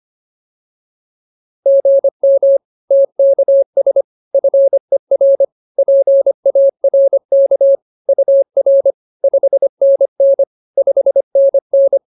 Morse code, heard as a keyed tone, about 24 wpm, 550 Hz. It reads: GM TKS FER PARK UR 5NN 5NN
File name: POTA309 25